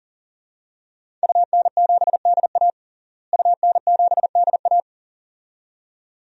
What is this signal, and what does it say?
Morse code, heard as a keyed tone, about 40 wpm, 700 Hz. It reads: UN7DA UN7DA
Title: DXCC-100 Call Signs - 5 Characters - Repeated 2x - 40wpm